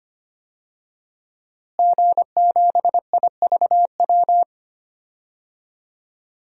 Morse code, heard as a keyed tone, about 25 wpm, 700 Hz. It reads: G7IVW
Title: DXCC-100 Call Signs - 5 Characters - Repeated 3x - 25wpm